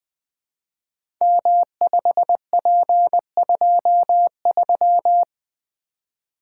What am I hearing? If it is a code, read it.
Morse code, heard as a keyed tone, about 20 wpm, 700 Hz. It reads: M5P23
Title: Code Groups - 5 Letter:Numbers - 20wpm